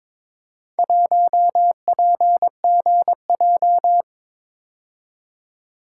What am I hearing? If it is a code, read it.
Morse code, heard as a keyed tone, about 22 wpm, 700 Hz. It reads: 1PGJ
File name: practice / 22wpm / Four Letter-Number Combinations - 22wpm